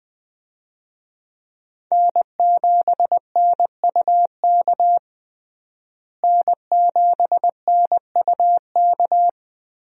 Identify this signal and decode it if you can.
Morse code, heard as a keyed tone, about 20 wpm, 700 Hz. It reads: N7NUK N7NUK
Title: USA Call Signs - 5 Characters - Repeated 3x - 20wpm